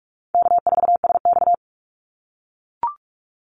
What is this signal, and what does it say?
Morse code, heard as a keyed tone, about 44 wpm, 700 Hz. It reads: K4SX
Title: Call Signs - 45wpm